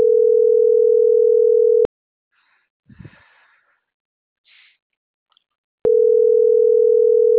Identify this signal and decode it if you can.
Ringback tone (North America), 440 + 480 Hz.